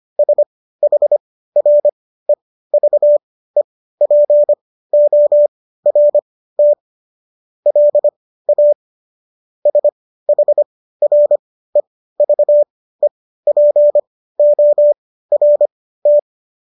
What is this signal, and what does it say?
Morse code, heard as a keyed tone, about 25 wpm, spaced as for 15 wpm, 590 Hz. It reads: SHREVEPORT LA SHREVEPORT